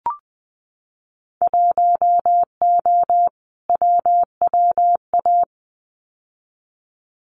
Morse code, heard as a keyed tone, about 20 wpm, 700 Hz. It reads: 1O WWA
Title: Winter Field Day Exchanges 20wpm